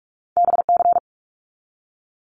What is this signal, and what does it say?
Morse code, heard as a keyed tone, about 46 wpm, 700 Hz. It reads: BC
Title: Canadian Provinces and Territories 45wpm